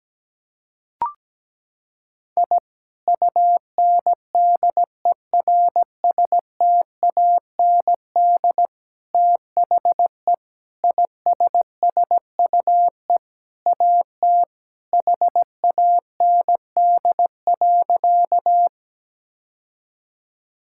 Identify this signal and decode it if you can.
Morse code, heard as a keyed tone, about 17 wpm, 700 Hz. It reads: I UNDERSTAND THE ISSUE AT HAND.
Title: Sentences from Top 1000 Words - Review Part 6 - 17wpm